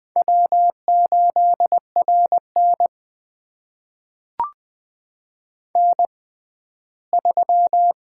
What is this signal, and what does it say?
Morse code, heard as a keyed tone, about 20 wpm, spaced as for 13 wpm, 700 Hz. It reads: W8RN N3
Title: USA Call Signs - 6x Character Spacing - 20wpm